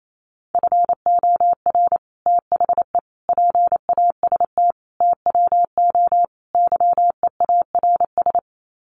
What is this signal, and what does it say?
Morse code, heard as a keyed tone, about 28 wpm, 700 Hz. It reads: FOR THE PAST TWO YEARS
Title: Sets of 5 Words Encore2 28wpm